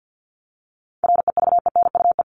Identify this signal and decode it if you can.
Morse code, heard as a keyed tone, about 48 wpm, 700 Hz. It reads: REVENUE
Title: Top 1501-1750 Words 50wpm